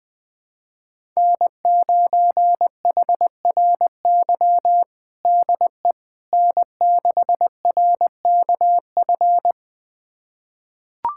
Morse code, heard as a keyed tone, about 20 wpm, 700 Hz. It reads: N9HRY DE N6RKF
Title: USA Callsign DE Callsign - 5 Characters - 20wpm